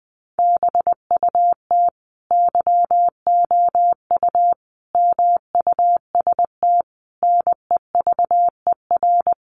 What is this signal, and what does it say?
Morse code, heard as a keyed tone, about 20 wpm, 700 Hz. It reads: BUT YOU MUST NEVER
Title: Sentences from Top 300 Words 20wpm